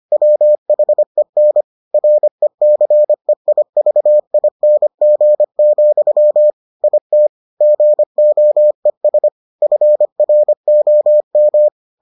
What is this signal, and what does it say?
Morse code, heard as a keyed tone, about 25 wpm, 600 Hz. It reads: WHEN RECEIVING, IT GOES FROM